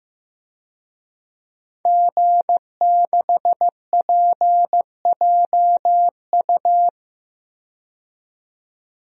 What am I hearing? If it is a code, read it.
Morse code, heard as a keyed tone, about 15 wpm, 700 Hz. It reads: G6PJU
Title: DXCC-100 Call Signs - 5 Characters - 15wpm